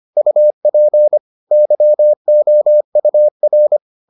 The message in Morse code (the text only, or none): UP YOUR